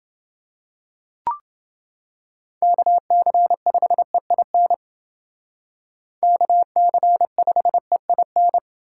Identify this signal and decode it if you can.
Morse code, heard as a keyed tone, about 30 wpm, 700 Hz. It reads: KC5EIN KC5EIN